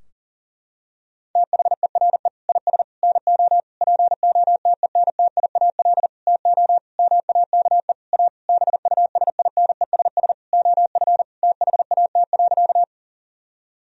Morse code, heard as a keyed tone, about 40 wpm, 700 Hz. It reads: THERE IS NO POTENTIAL TO MAKE A BUSINESS OF THAT.
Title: Sentences from Top 1000 Words - Review Part 4 - 40wpm